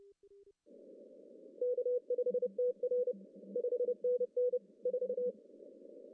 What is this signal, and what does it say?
Morse code, heard as a keyed tone, about 30 wpm, 500 Hz. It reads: K5TR 5NN 4
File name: CW / cqww-cw-k5tr